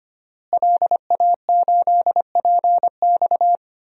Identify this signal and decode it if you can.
Morse code, heard as a keyed tone, about 25 wpm, 700 Hz. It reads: LA8PX
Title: DXCC-100 Call Signs - 5 Characters - Repeated 3x - 25wpm